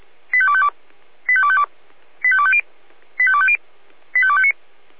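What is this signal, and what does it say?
Five-tone selective call (ZVEI), heard as quick runs of five tones, tones of about 70 ms. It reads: 85252 85252 85250 85250 85259
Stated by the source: Five-tone ZVEI selective call 85202, received by SDR.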